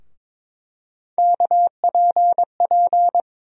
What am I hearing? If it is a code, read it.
Morse code, heard as a keyed tone, about 22 wpm, 700 Hz. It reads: KPP
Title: USA Call Sign Postfixes - 3 Letters 22wpm